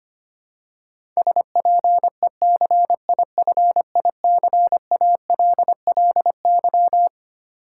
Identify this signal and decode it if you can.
Morse code, heard as a keyed tone, about 25 wpm, 700 Hz. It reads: SPECIFICALLY
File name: Twelve Letter Words - 25wpm